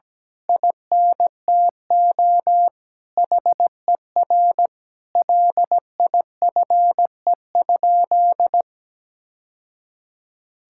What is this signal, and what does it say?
Morse code, heard as a keyed tone, about 17 wpm, 700 Hz. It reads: INTO HER LIFE?
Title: Sentences from Top 1000 Words - Review Part 6 - 17wpm